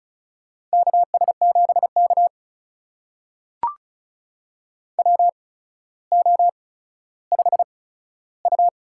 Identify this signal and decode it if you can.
Morse code, heard as a keyed tone, about 36 wpm, spaced as for 15 wpm, 700 Hz. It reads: KS7K WO5U
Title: USA Call Signs - 4 Characters - 8x Character Spacing - 35wpm